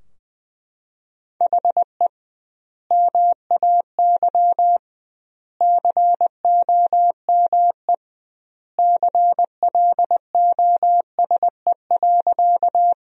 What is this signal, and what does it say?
Morse code, heard as a keyed tone, about 20 wpm, 700 Hz. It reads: HE MAY COME CLOSE.
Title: 2x Word-Spacing - Sentences from Top 300 Words - 20wpm